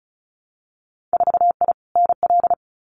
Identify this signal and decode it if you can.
Morse code, heard as a keyed tone, about 36 wpm, 700 Hz. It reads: 4I NL
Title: Winter Field Day Exchanges Encore 35wpm